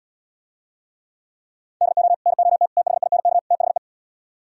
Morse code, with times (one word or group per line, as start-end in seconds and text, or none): 1.8-2.1 DO
2.3-2.7 NOT
2.8-3.4 THINK
3.5-3.8 THE